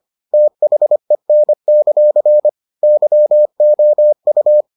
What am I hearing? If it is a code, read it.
Morse code, heard as a keyed tone, about 25 wpm, 600 Hz. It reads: THEN; YOU